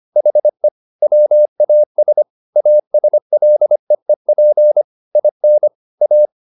Morse code, heard as a keyed tone, about 25 wpm, 600 Hz. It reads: HE WAS ASLEEP IN A
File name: the old man and the sea part 1 25wpm